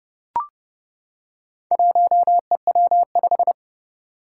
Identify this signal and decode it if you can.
Morse code, heard as a keyed tone, about 30 wpm, 700 Hz. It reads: 1EW5
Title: Four Challenging Letter-Number Combinations 30wpm